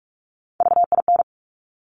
Morse code, heard as a keyed tone, about 46 wpm, 700 Hz. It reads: VIN
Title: Top Three Letter Combinations in Words 45wpm